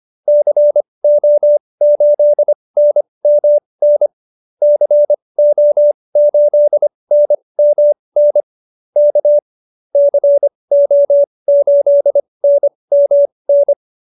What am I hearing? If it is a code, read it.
Morse code, heard as a keyed tone, about 25 wpm, spaced as for 20 wpm, 590 Hz. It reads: CO8NMN CO8NMN K CO8NMN